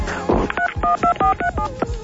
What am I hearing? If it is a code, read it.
DTMF, heard as telephone touch tones, tones of about 100 ms, with speech. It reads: 312437